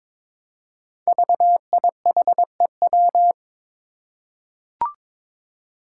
Morse code, heard as a keyed tone, about 22 wpm, 700 Hz. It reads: VIHEW
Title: Code Groups - 5 Letter:Numbers - 22wpm